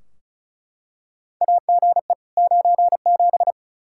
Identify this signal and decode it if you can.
Morse code, heard as a keyed tone, about 36 wpm, 700 Hz. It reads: AGE 97
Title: Practice QSO - Exchange 3 - More Info - 35wpm